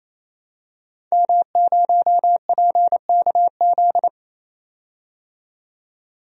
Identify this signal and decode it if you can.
Morse code, heard as a keyed tone, about 28 wpm, 700 Hz. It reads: M0PKZ
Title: DXCC-100 Call Signs - 5 Characters - 28wpm